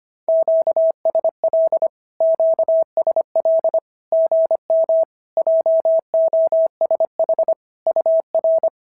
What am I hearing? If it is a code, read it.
Morse code, heard as a keyed tone, about 23 wpm, 650 Hz. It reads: QSL QSL GM JOSH UR
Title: SOTA QSO Practice 25wpm